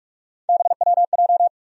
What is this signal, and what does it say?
Morse code, heard as a keyed tone, about 47 wpm, 700 Hz. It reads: BWJ
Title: USA Call Sign Postfixes - 3 Letters 45wpm